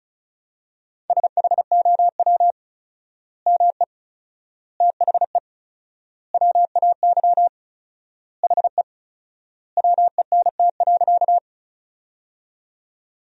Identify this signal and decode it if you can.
Morse code, heard as a keyed tone, about 36 wpm, 700 Hz. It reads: SHOW ME THE WAY HE WENT.
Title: 4x Word-Spacing - Sentences from Top 200 Words - 35wpm